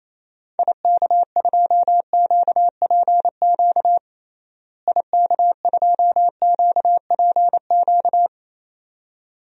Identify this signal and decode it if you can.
Morse code, heard as a keyed tone, about 28 wpm, 700 Hz. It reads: IK2QPQ IK2QPQ
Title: DXCC-100 Call Signs - 6 Characters - Repeated 2x - 28wpm